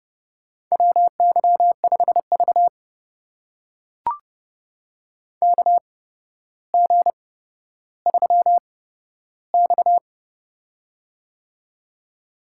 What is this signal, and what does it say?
Morse code, heard as a keyed tone, about 29 wpm, spaced as for 13 wpm, 700 Hz. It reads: WY5V KG3X
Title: USA Call Signs - 4 Characters - 8x Character Spacing - 30wpm